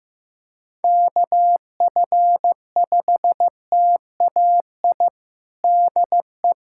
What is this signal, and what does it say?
Morse code, heard as a keyed tone, about 15 wpm, 700 Hz. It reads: KF5TAI DE
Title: USA Callsign DE Callsign 15wpm